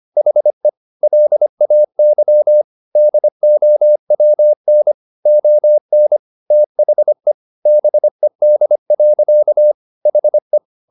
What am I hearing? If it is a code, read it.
Morse code, heard as a keyed tone, about 25 wpm, 600 Hz. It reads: HE LAY DOWN ON THE BED. HE